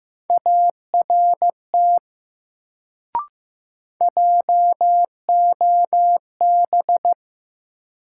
Morse code, heard as a keyed tone, about 15 wpm, 700 Hz. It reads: ART JOB
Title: Sentences from Top 600 Words - New Word Drill - 15wpm